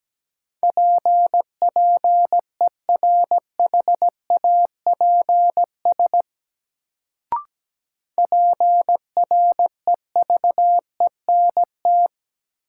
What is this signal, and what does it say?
Morse code, heard as a keyed tone, about 17 wpm, 700 Hz. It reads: PPERHAPS PREVENT
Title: Sentences from Top 1100 Words - New Word Drill - 17wpm